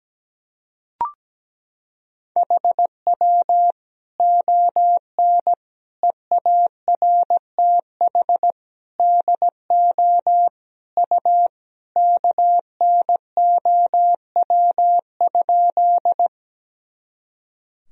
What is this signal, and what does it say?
Morse code, heard as a keyed tone, about 17 wpm, 700 Hz. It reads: HW ON EARTH DO U KNOW?
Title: Sentences from Top 200 Words and Abbreviations 17wpm